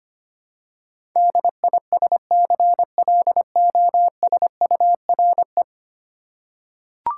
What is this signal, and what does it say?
Morse code, heard as a keyed tone, about 25 wpm, 700 Hz. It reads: DISCLOSURE
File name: Ten Letter Words - 25wpm